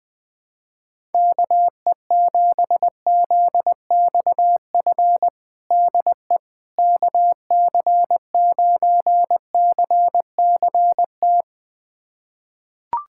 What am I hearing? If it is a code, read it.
Morse code, heard as a keyed tone, about 20 wpm, 700 Hz. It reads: KE7ZXF DE KC9CCT
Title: USA Callsign DE Callsign - 6 Characters - 20wpm